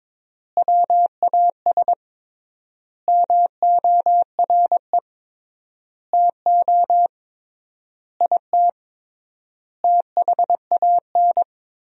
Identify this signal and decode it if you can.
Morse code, heard as a keyed tone, about 22 wpm, 700 Hz. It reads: WAS MORE TO IT THAN
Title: 3x Word-Spacing - Sentences from Top 100 Words - 22wpm